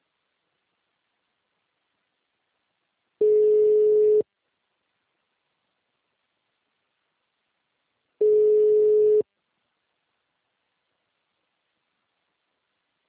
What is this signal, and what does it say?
Ringback tone (Europe, 425 Hz).
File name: out- -2002- - - .4885